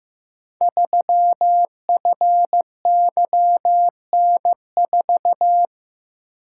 Morse code, heard as a keyed tone, about 15 wpm, 700 Hz. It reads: 3FYN4
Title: Code Groups - 5 Letter:Numbers - 15wpm